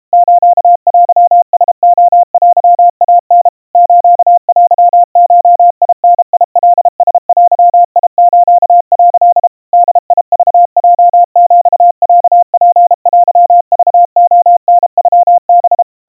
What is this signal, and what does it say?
Morse code (Japanese Wabun code), heard as a keyed tone, about 32 wpm, 700 Hz. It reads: ステラレテイタ ステゴダカラデス」 ボクヲヒロツテクレタノハ
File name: kawa / Morse sample / kamyu 120 cpm 700 Hz Crispy